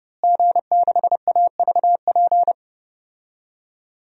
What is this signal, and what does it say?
Morse code, heard as a keyed tone, about 32 wpm, 700 Hz. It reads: G6AVP
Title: DXCC-100 Call Signs - 5 Characters - Repeated 3x - 30wpm